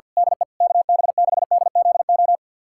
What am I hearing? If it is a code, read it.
Morse code, heard as a keyed tone, about 52 wpm, 700 Hz. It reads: DE KB6DZO
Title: USA Callsign DE Callsign 50wpm